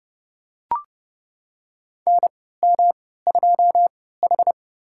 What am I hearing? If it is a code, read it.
Morse code, heard as a keyed tone, about 30 wpm, spaced as for 17 wpm, 700 Hz. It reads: NM2H
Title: USA Call Signs - 4 Characters - 3x Character Spacing - 30wpm